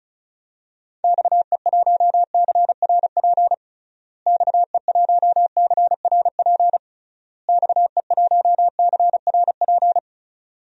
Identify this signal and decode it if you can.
Morse code, heard as a keyed tone, about 34 wpm, 700 Hz. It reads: XE1CRP XE1CRP XE1CRP
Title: DXCC-100 Call Signs - 6 Characters - Repeated 3x - 35wpm